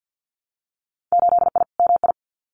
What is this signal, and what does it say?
Morse code, heard as a keyed tone, about 50 wpm, 700 Hz. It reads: 8I MI
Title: Winter Field Day Exchanges Encore 50wpm